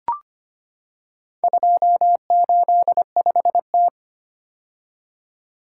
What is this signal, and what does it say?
Morse code, heard as a keyed tone, about 25 wpm, 700 Hz. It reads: 285T
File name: practice / 25wpm / Four Letter-Number Combinations - 25wpm